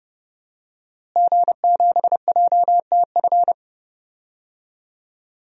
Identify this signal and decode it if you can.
Morse code, heard as a keyed tone, about 30 wpm, 700 Hz. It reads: G7JTF
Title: DXCC-100 Call Signs - 5 Characters - Repeated 3x - 30wpm